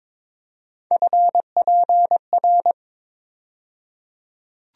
Morse code, heard as a keyed tone, about 22 wpm, 700 Hz. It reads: FPR
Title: USA Call Sign Postfixes - 3 Letters 22wpm